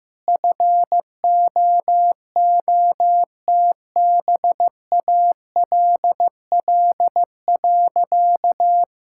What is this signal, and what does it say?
Morse code, heard as a keyed tone, about 15 wpm, 700 Hz. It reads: FOOTBALL.
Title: Sentences from Top 1600 Words 15wpm